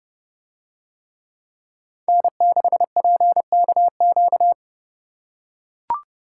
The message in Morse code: N6PKQ

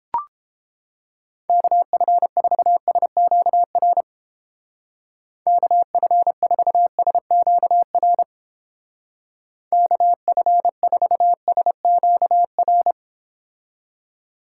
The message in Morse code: KF4SQR KF4SQR KF4SQR

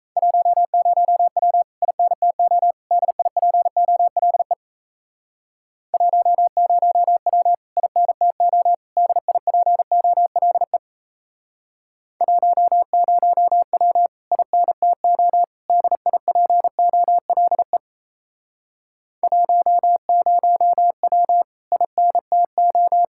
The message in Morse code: 10W INTO DIPOLE 10W INTO DIPOLE 10W INTO DIPOLE 10W INTO

